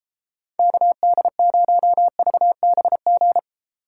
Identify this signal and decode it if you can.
Morse code, heard as a keyed tone, about 36 wpm, 700 Hz. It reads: KD0VBG